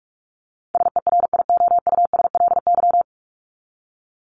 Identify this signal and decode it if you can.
Morse code, heard as a keyed tone, about 44 wpm, 700 Hz. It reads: SERIOUSLY
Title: LY Suffix - Top 100 Words with Base Words 45wpm